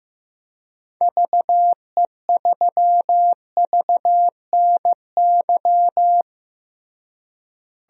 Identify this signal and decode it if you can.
Morse code, heard as a keyed tone, about 15 wpm, 700 Hz. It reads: VE3VNY
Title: DXCC-100 Call Signs - 6 Characters - Repeated 3x - 15wpm